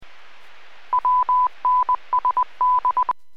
Morse code, heard as a keyed tone, about 20 wpm, 1000 Hz. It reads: WNSD